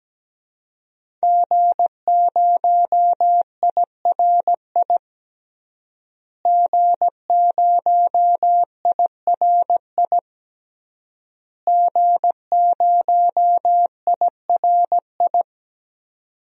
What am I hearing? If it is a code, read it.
Morse code, heard as a keyed tone, about 17 wpm, 700 Hz. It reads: G0IRI G0IRI G0IRI
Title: DXCC-100 Call Signs - 5 Characters - Repeated 3x - 17wpm